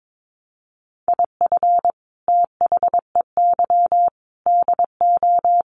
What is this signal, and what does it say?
Morse code, heard as a keyed tone, about 22 wpm, 700 Hz. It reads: IF THEY DO
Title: Sets of 4 Words 22wpm